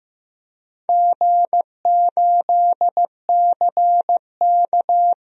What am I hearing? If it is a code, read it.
Morse code, heard as a keyed tone, about 15 wpm, 700 Hz. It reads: G8CK
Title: DXCC-100 Call Signs - 4 Characters - 15wpm